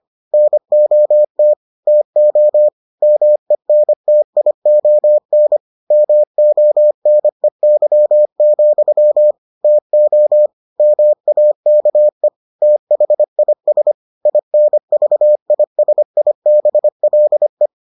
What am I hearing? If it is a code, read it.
Morse code, heard as a keyed tone, about 25 wpm, 600 Hz. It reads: NOT TO MENTION MONEY, TO MAKE THIS INVISIBLE